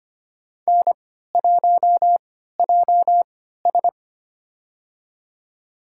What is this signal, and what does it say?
Morse code, heard as a keyed tone, about 25 wpm, spaced as for 14 wpm, 700 Hz. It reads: N1JS
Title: USA Call Signs - 3x Character Spacing - 25wpm